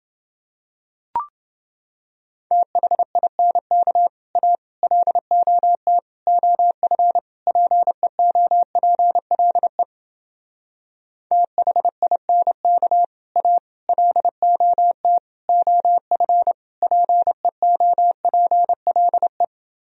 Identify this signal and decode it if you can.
Morse code, heard as a keyed tone, about 30 wpm, 700 Hz. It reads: THINK A LOT OF PEOPLE THINK A LOT OF PEOPLE